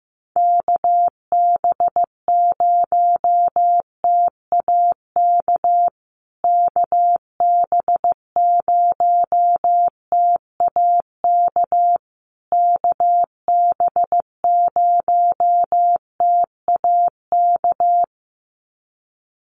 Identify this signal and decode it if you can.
Morse code, heard as a keyed tone, about 15 wpm, 700 Hz. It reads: KB0TAK KB0TAK KB0TAK
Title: USA Call Signs Repeated 3x 15wpm